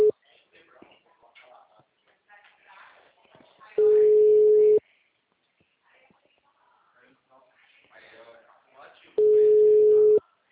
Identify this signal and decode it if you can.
Ringback tone (Europe, 425 Hz), with speech.